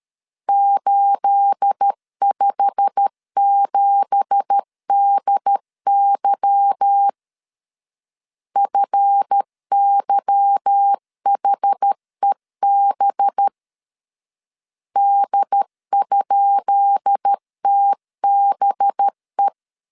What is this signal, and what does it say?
Morse code, heard as a keyed tone, about 13 wpm, 800 Hz. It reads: 857DY FYHEB D?TBE
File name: LEZIONI / 10 SIGNS / 10B INT 12wpm 2414